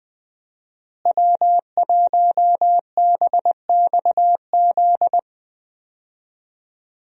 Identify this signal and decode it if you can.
Morse code, heard as a keyed tone, about 20 wpm, 700 Hz. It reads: W1BXZ
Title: USA Call Signs - 5 Characters - Repeated 2x - 20wpm